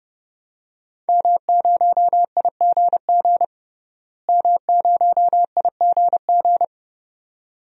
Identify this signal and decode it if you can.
Morse code, heard as a keyed tone, about 31 wpm, 700 Hz. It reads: M0IGG M0IGG
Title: DXCC-100 Call Signs - 5 Characters - Repeated 2x - 30wpm